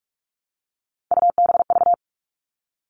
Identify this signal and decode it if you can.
Morse code, heard as a keyed tone, about 44 wpm, 700 Hz. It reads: UBV
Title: All Cut Numbers - 3-Digit 45wpm